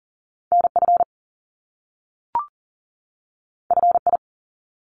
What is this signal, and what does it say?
Morse code, heard as a keyed tone, about 41 wpm, 700 Hz. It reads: NF FI
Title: Top Two Letter Combinations in Words 40wpm